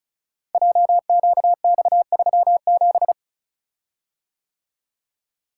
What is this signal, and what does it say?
Morse code, heard as a keyed tone, about 36 wpm, 700 Hz. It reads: JQX37
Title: Code Groups - 5 Letter:Numbers - 35wpm